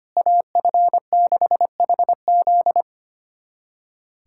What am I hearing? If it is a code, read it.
Morse code, heard as a keyed tone, about 25 wpm, 700 Hz. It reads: AF6HZ
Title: USA Callsign DE Callsign 25wpm